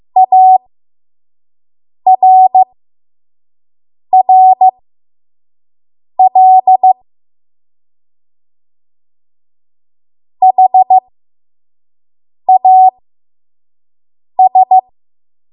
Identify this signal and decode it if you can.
Morse code, heard as a keyed tone, about 15 wpm, spaced as for 5 wpm, 750 Hz. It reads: ARRL HAS
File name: Morse / 05WPM